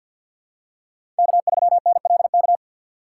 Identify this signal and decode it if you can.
Morse code, heard as a keyed tone, about 53 wpm, 700 Hz. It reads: K3NLK
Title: USA Call Signs - 5 Characters - 3x Character Spacing - 50wpm